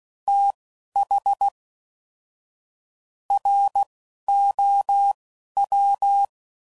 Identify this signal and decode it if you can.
Morse code, heard as a keyed tone, about 16 wpm, spaced as for 11 wpm, 800 Hz. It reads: TH ROW
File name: Prefixes / FLs & THs / Fast TH Prefixes